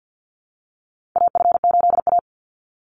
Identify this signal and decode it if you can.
Morse code, heard as a keyed tone, about 50 wpm, 700 Hz. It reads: AF8A
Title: Call Signs - 50wpm